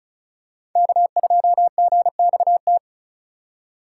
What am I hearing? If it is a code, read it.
Morse code, heard as a keyed tone, about 36 wpm, 700 Hz. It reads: K2GXT
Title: USA Call Signs - 4x Character Spacing - 35wpm